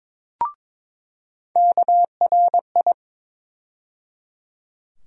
Morse code, heard as a keyed tone, about 22 wpm, 700 Hz. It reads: KRI
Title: USA Call Sign Postfixes - 3 Letters 22wpm